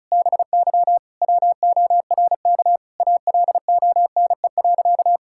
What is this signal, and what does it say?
Morse code, heard as a keyed tone, about 35 wpm, 700 Hz. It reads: BY WORK ALONE.
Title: Sentences from Top 1600 Words 35wpm